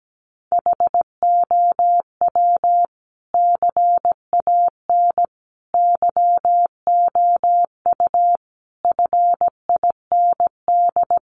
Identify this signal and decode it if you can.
Morse code, heard as a keyed tone, about 17 wpm, 700 Hz. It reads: HOW CAN YOU FIND
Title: Sentences from Top 200 Words 17wpm